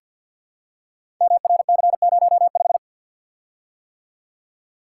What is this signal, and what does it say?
Morse code, heard as a keyed tone, about 52 wpm, 700 Hz. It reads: MRC05
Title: Code Groups - 5 Letter:Numbers - 50wpm